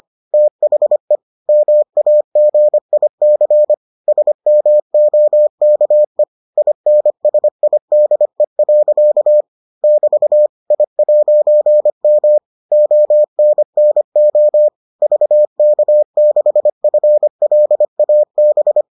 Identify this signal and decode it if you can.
Morse code, heard as a keyed tone, about 25 wpm, 600 Hz. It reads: THE MAGIC SMOKE INSIDE. <BT> I'M ONNO VK6FLAB